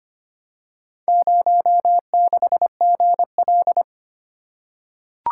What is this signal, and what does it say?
Morse code, heard as a keyed tone, about 25 wpm, 700 Hz. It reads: 06GL
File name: practice / 25wpm / Four Letter-Number Combinations - 25wpm